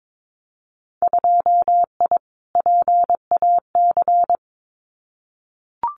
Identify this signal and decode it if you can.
Morse code, heard as a keyed tone, about 22 wpm, 700 Hz. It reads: 2I PAC
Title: Winter Field Day Exchanges Encore 22wpm